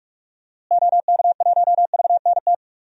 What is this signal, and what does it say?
Morse code, heard as a keyed tone, about 43 wpm, 700 Hz. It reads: OK1VNT
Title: DXCC-100 Call Signs - 6 Characters - Repeated 2x - 45wpm